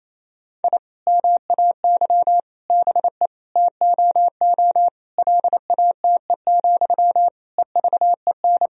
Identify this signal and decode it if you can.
Morse code, heard as a keyed tone, about 28 wpm, 700 Hz. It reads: I MAY BE TOO LATE, EVEN
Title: Sentences from Top 1000 Words - Review Part 1 - 28wpm